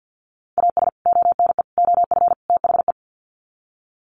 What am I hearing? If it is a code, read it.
Morse code, heard as a keyed tone, about 50 wpm, 700 Hz. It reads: AS ONE OF THE
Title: Sets of 4 Words 50wpm